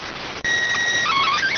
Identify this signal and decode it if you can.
SSTV (slow-scan television), PD 290.